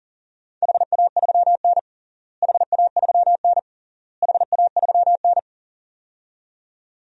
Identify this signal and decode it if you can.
Morse code, heard as a keyed tone, about 40 wpm, 700 Hz. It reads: HA3N HA3N HA3N
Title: DXCC-100 Call Signs - 4 Characters - Repeated 3x - 40wpm